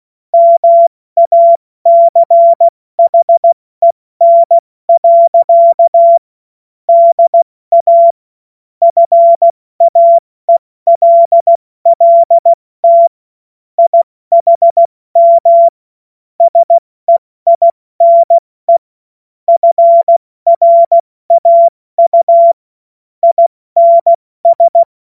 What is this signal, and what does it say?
Morse code, heard as a keyed tone, about 16 wpm, spaced as for 14 wpm, 680 Hz. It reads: MACHEN. DA FAELLT IHM SEINE FRAU INS
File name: Morsekurs / Witze / Witz 28. Dezember 2021, 70 F80